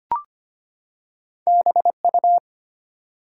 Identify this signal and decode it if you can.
Morse code, heard as a keyed tone, about 25 wpm, 700 Hz. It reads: BU
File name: Two Letter-Number Combinations - 25wpm